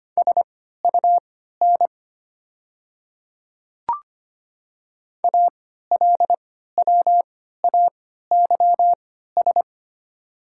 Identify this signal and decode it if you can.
Morse code, heard as a keyed tone, about 25 wpm, spaced as for 14 wpm, 700 Hz. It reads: SUN ALWAYS
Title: 3x Character Spacing - Top 300 Words - 25wpm